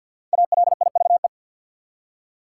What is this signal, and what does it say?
Morse code, heard as a keyed tone, about 51 wpm, 700 Hz. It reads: ALIVE
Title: Five Letter Words - 50wpm